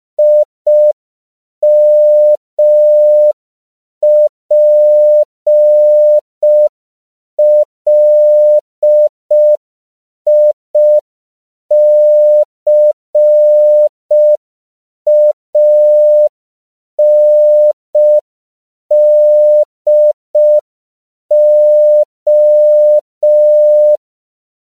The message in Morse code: IMPLICANDO

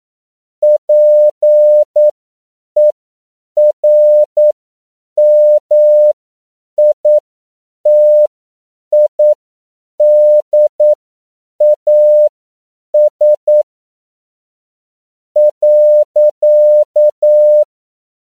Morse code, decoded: PERMITIDAS.

